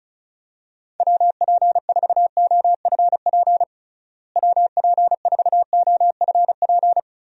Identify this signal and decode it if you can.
Morse code, heard as a keyed tone, about 36 wpm, 700 Hz. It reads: WP4OFP WP4OFP
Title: USA Call Signs - 6 Characters - Repeated 2x - 35wpm